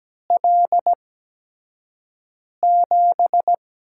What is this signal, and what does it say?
Morse code, heard as a keyed tone, about 17 wpm, spaced as for 5 wpm, 700 Hz. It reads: L7
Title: USA Call Signs - 5 Characters - 8x Character Spacing - 17wpm